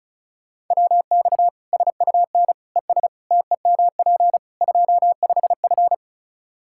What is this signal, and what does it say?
Morse code, heard as a keyed tone, about 36 wpm, 700 Hz. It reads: WX SUN ES TEMP 25F
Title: Practice QSO - Exchange 2 - Info - 35wpm